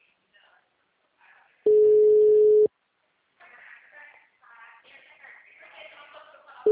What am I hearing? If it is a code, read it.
Ringback tone (Europe, 425 Hz), with speech.